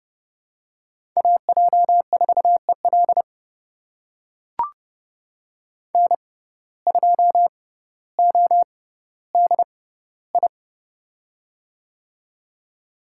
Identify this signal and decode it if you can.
Morse code, heard as a keyed tone, about 30 wpm, spaced as for 15 wpm, 700 Hz. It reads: AJ4EL N2ODI
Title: USA Call Signs - 6x Character Spacing - 30wpm